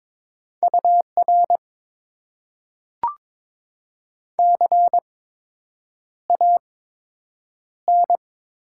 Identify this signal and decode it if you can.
Morse code, heard as a keyed tone, about 22 wpm, spaced as for 8 wpm, 700 Hz. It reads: UR CAN
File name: Top 90 QSO Elements - 8x Character Spacing - 22wpm